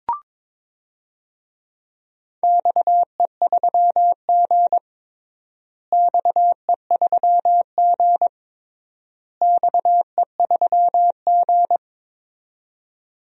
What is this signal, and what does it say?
Morse code, heard as a keyed tone, about 22 wpm, 700 Hz. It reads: XE3G XE3G XE3G
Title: DXCC-100 Call Signs - 4 Characters - Repeated 3x - 22wpm